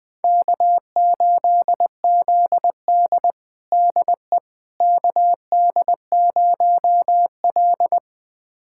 Morse code, heard as a keyed tone, about 20 wpm, 700 Hz. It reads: K8ZD DE KD0L
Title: USA Callsign DE Callsign - 4 Characters - 20wpm